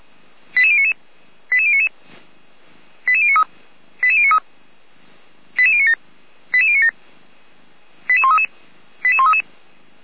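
Five-tone selective call (ZVEI), heard as quick runs of five tones, tones of about 70 ms. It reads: 88099 88099 88093 88093 88097 88097 88133 88133
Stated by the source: ZVEI five-tone selective call received with an SDR, fire brigade Feuerwehr Orlamünde.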